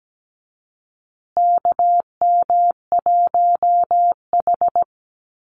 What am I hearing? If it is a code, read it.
Morse code, heard as a keyed tone, about 17 wpm, 700 Hz. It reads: KM1H
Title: USA Call Signs Repeated 3x 17wpm